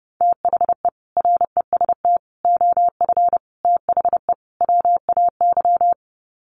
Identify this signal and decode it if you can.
Morse code, heard as a keyed tone, about 30 wpm, 700 Hz. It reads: THE REST OF THE WAY
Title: Sets of 5 Words Encore 30wpm